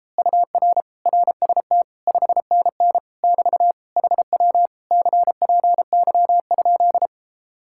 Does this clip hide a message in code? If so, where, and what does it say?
Yes: Morse code from 0.2 to 7.1 s, heard as a keyed tone, about 33 wpm, 700 Hz. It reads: UR RST 5NN <BT> HW CPY?